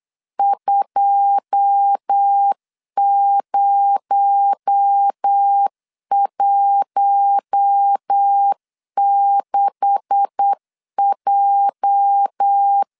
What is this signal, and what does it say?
Morse code, heard as a keyed tone, about 8 wpm, 800 Hz. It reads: 2016J
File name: CW / DET / 1 9 1500C S A 8wpm 5637